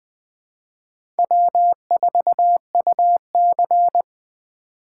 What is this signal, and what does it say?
Morse code, heard as a keyed tone, about 21 wpm, 700 Hz. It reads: W4UC
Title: USA Call Signs - 4 Characters - 4x Character Spacing - 20wpm